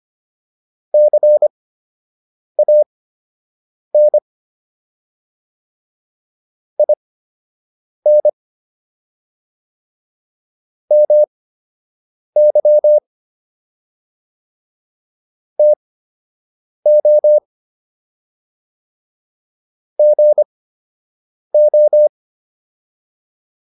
Morse code, heard as a keyed tone, about 25 wpm, spaced as for 7 wpm, 600 Hz. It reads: CAN IN MY TO GO